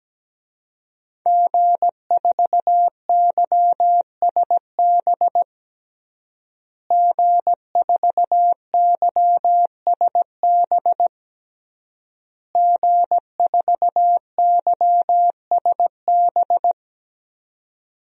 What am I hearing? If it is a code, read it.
Morse code, heard as a keyed tone, about 17 wpm, 700 Hz. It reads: G4YSB G4YSB G4YSB
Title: DXCC-100 Call Signs - 5 Characters - Repeated 3x - 17wpm